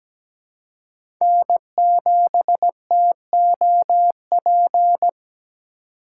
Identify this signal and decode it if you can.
Morse code, heard as a keyed tone, about 17 wpm, 700 Hz. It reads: N7TOP
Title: USA Call Signs - 4x Character Spacing - 17wpm